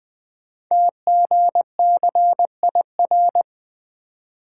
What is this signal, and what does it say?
Morse code, heard as a keyed tone, about 20 wpm, 700 Hz. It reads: TGCIR
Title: Code Groups - 5 Letters - 20wpm